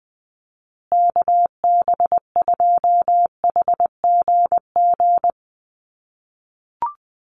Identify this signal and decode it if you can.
Morse code, heard as a keyed tone, about 20 wpm, 700 Hz. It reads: KB2HGG
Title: USA Call Signs Repeated 2x 20wpm